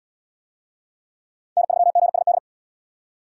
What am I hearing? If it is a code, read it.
Morse code, heard as a keyed tone, about 76 wpm, 700 Hz. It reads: N3GSD